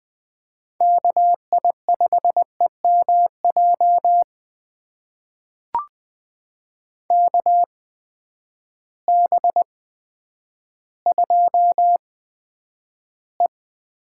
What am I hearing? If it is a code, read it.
Morse code, heard as a keyed tone, about 20 wpm, spaced as for 10 wpm, 700 Hz. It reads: KI5EMJ KB2E
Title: USA Call Signs - 6 Characters - 8x Character Spacing - 20wpm